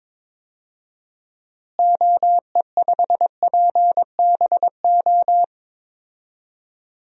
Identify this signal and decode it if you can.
Morse code, heard as a keyed tone, about 22 wpm, 700 Hz. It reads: OE5PBO
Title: DXCC-100 Call Signs - 6 Characters - Repeated 2x - 22wpm